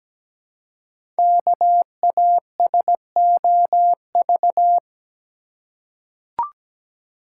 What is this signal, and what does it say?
Morse code, heard as a keyed tone, about 17 wpm, 700 Hz. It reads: KASOV
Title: Code Groups - 5 Letters - 17wpm